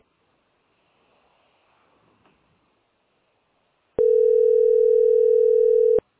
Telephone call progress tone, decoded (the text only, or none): ringback tone